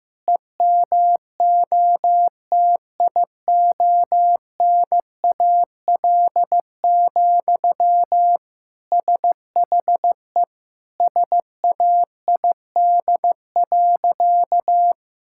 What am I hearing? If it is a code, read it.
Morse code, heard as a keyed tone, about 16 wpm, 700 Hz. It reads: EMOTIONAL, SHE SAID.
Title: Sentences from Top 1000 Words - Review Part 5 - 15wpm